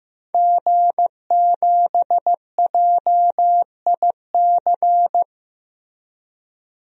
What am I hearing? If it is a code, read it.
Morse code, heard as a keyed tone, about 16 wpm, 700 Hz. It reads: G7JIC
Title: DXCC-100 Call Signs - 5 Characters - Repeated 2x - 15wpm